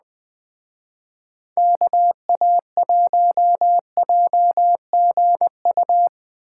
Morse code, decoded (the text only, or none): KA1JGU